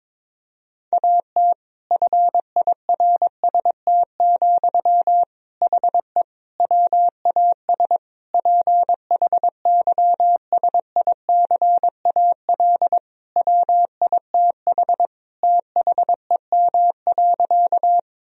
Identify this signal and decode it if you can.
Morse code, heard as a keyed tone, about 22 wpm, 700 Hz. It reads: AT FIRST, HE WAS PHYSICAL WITH THEM.
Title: Sentences from Top 1000 Words - Review Part 4 - 22wpm